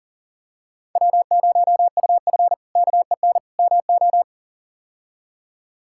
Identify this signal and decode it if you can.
Morse code, heard as a keyed tone, about 37 wpm, 700 Hz. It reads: W0UF KEN MO
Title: K1USN SST QSO Practice - 40wpm